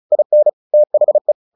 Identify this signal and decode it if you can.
Morse code, heard as a keyed tone, about 34 wpm, 600 Hz. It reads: IN THE
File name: the old man and the sea / the old man and the sea 1 35wpm